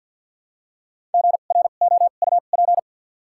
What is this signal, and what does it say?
Morse code, heard as a keyed tone, about 53 wpm, spaced as for 37 wpm, 700 Hz. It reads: GROUP
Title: 2x Character Spacing - Top 300 Words - 50wpm